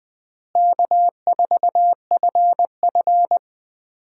Morse code, heard as a keyed tone, about 20 wpm, 690 Hz. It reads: K4FF
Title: USA Call Signs - 4 Characters - Repeated 3x - 20wpm